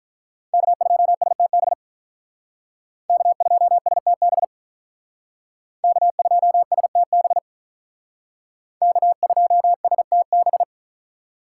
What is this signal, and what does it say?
Morse code, heard as a keyed tone, about 44 wpm, 700 Hz. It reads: K2STB K2STB K2STB K2STB